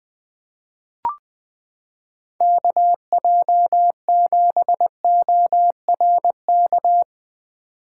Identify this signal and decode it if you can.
Morse code, heard as a keyed tone, about 20 wpm, 700 Hz. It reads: KJ7ORK
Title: 2023 POTA Activator Call Signs - 20wpm